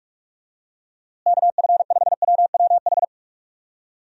Morse code, heard as a keyed tone, about 46 wpm, 700 Hz. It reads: KF5WWH
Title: USA Call Signs - 6 Characters - Repeated 2x - 45wpm